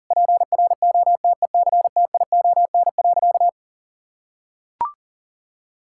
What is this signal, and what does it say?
Morse code, heard as a keyed tone, about 37 wpm, 700 Hz. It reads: PROTECTION.
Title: Sentences from Top 1600 Words 40wpm